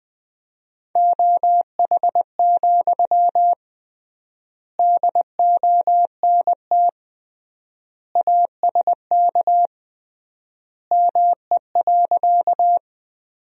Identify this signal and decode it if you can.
Morse code, heard as a keyed tone, about 20 wpm, 700 Hz. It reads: OH, DONT ASK ME.
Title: 3x Word-Spacing - Sentences from Top 500 Words - 20wpm